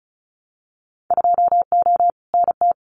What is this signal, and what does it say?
Morse code, heard as a keyed tone, about 36 wpm, 700 Hz. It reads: 2O NT
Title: Winter Field Day Exchanges Encore 35wpm